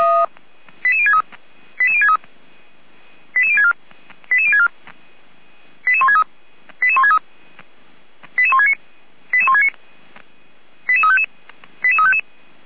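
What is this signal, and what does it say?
Five-tone selective call (ZVEI), heard as quick runs of five tones, tones of about 70 ms. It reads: 88063 88063 88064 88064 88163 88163 88169 88169 88355 88355
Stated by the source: Five-tone ZVEI selective call received by SDR.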